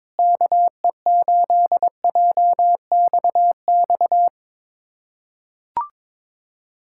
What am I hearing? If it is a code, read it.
Morse code, heard as a keyed tone, about 22 wpm, 700 Hz. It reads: KE8JXX
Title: USA Callsign DE Callsign - 6 Characters - 22wpm